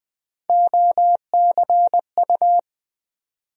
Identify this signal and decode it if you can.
Morse code, heard as a keyed tone, about 20 wpm, 700 Hz. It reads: OCU
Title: USA Call Sign Postfixes - 3 Letters 20wpm